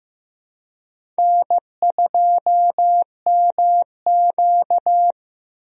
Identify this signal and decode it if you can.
Morse code, heard as a keyed tone, about 15 wpm, 700 Hz. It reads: N2MQ
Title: USA Call Signs - 4 Characters - Repeated 3x - 15wpm